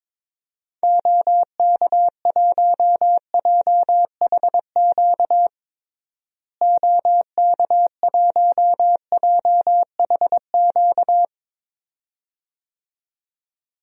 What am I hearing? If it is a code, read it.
Morse code, heard as a keyed tone, about 22 wpm, 700 Hz. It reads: OK1JHQ OK1JHQ
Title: DXCC-100 Call Signs - 6 Characters - Repeated 2x - 22wpm